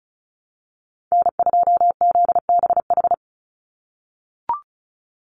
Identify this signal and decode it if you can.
Morse code, heard as a keyed tone, about 34 wpm, 700 Hz. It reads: N2ZBH
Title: USA Call Signs 35wpm